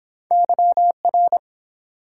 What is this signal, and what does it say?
Morse code, heard as a keyed tone, about 30 wpm, 700 Hz. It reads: YR